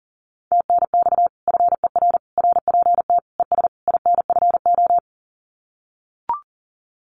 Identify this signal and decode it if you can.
Morse code, heard as a keyed tone, about 40 wpm, 700 Hz. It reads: TNX FER RPT ES INFO
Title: Common QSO Phrases Encore 40wpm